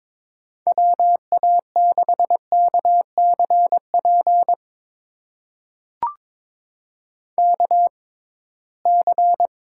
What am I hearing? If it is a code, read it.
Morse code, heard as a keyed tone, about 22 wpm, spaced as for 17 wpm, 700 Hz. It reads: WA6KCP KC
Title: USA Call Signs - 6x Character Spacing - 22wpm